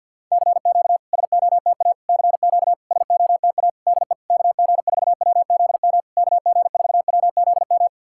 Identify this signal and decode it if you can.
Morse code, heard as a keyed tone, about 50 wpm, 700 Hz. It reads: CQ SOTA CQ SOTA DE KG4WZM KG4WZM
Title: SOTA QSO Practice 50wpm